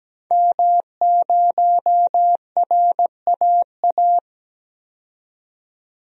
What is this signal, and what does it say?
Morse code, heard as a keyed tone, about 17 wpm, 700 Hz. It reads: M0RAA
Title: DXCC-100 Call Signs - 5 Characters - 17wpm